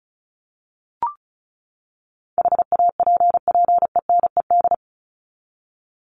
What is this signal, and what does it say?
Morse code, heard as a keyed tone, about 34 wpm, 700 Hz. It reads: HAPPENED
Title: ED Suffix - Top 100 Words with Base Words 35wpm